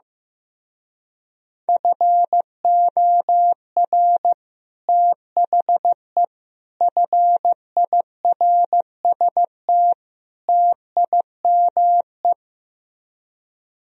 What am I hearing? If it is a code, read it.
Morse code, heard as a keyed tone, about 15 wpm, 700 Hz. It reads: FOR THE FIRST TIME
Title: Speed-Racing - Sets of 4 Words 15wpm